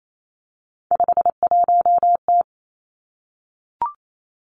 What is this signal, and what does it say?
Morse code, heard as a keyed tone, about 28 wpm, 700 Hz. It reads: 51T
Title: Three Challenging Letter-Number Combinations 28wpm